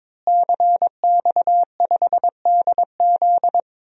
Morse code, heard as a keyed tone, about 22 wpm, 700 Hz. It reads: CX5DZ
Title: DXCC-100 Call Signs - 5 Characters - Repeated 3x - 22wpm